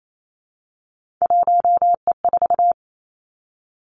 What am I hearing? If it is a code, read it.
Morse code, heard as a keyed tone, about 28 wpm, 700 Hz. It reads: 1E4
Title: Three Challenging Letter-Number Combinations 28wpm